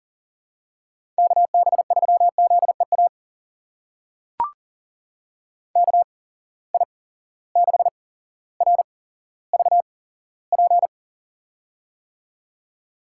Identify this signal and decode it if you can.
Morse code, heard as a keyed tone, about 39 wpm, spaced as for 17 wpm, 700 Hz. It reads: KB3ZEA KI6RVP
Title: USA Call Signs - 6 Characters - 8x Character Spacing - 40wpm